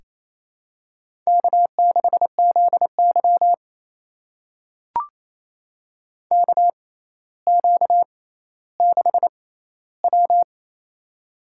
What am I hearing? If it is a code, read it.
Morse code, heard as a keyed tone, about 28 wpm, spaced as for 14 wpm, 700 Hz. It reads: K6ZY KQ6W
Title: USA Call Signs - 4 Characters - 6x Character Spacing - 28wpm